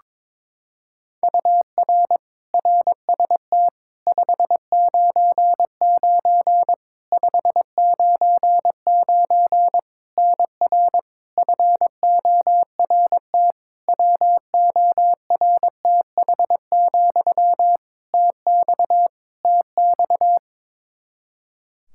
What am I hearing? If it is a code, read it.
Morse code, heard as a keyed tone, about 21 wpm, 700 Hz. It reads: UR RST 599 599 NR FORT WORTH, TX TX
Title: Practice QSO - Exchange 1 - Beginning - 22wpm